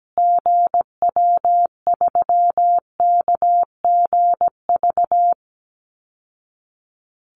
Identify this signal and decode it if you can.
Morse code, heard as a keyed tone, about 17 wpm, 700 Hz. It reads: GW3KGV
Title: Call Signs Repeated 3x - 17wpm